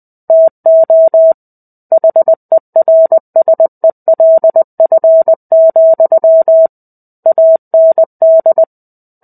Morse code, heard as a keyed tone, about 20 wpm, 650 Hz. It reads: TO HERSELF, AND
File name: 05-Chapter-05 20 WPM